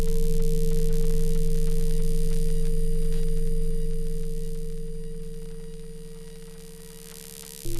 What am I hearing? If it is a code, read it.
Dial tone (China, 450 Hz).